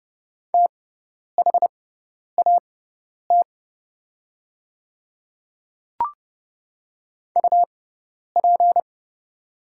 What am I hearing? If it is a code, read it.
Morse code, heard as a keyed tone, about 29 wpm, spaced as for 10 wpm, 700 Hz. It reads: THAT UP
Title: 6x Character Spacing - Top 100 Words - 30wpm